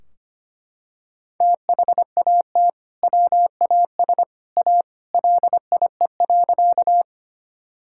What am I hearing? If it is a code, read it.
Morse code, heard as a keyed tone, about 25 wpm, 700 Hz. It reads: THAT WAS A LIE.
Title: Sentences from Top 1600 Words 25wpm